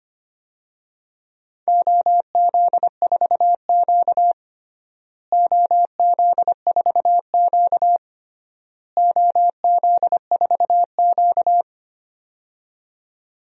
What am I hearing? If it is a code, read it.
Morse code, heard as a keyed tone, about 25 wpm, 700 Hz. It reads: OZ4Q OZ4Q OZ4Q
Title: DXCC-100 Call Signs - 4 Characters - Repeated 3x - 25wpm